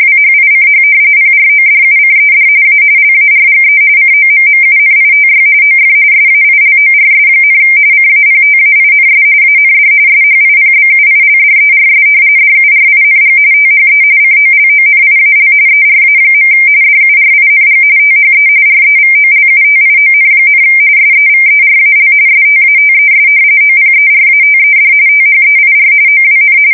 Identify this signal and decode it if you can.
RTTY, about 45.45 baud. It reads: WHICH PRODUCED A LARGE COMMUNITY OF SAILORS, CAPTAINS AND SHIPBUILDERS DURING THE COLONIAL ERA. DURING THE REVOLUTION, SEVERAL STATES OPERATED THEIR OWN
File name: fsk170 / usnavy1